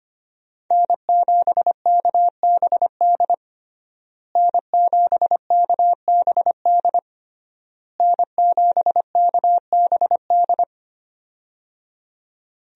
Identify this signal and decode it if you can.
Morse code, heard as a keyed tone, about 25 wpm, 700 Hz. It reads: N7KBD N7KBD N7KBD
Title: USA Call Signs - 5 Characters - Repeated 3x - 25wpm